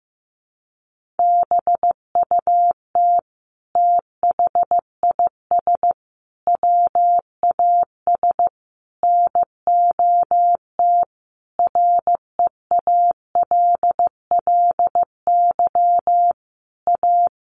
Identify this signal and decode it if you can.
Morse code, heard as a keyed tone, about 15 wpm, 700 Hz. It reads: BUT THIS WAS NOT REALLY A
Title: Sentences from Top 1500 Words 15wpm